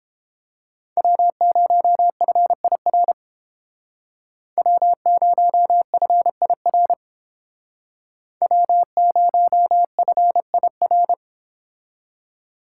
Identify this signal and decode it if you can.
Morse code, heard as a keyed tone, about 28 wpm, 700 Hz. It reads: W0FIR W0FIR W0FIR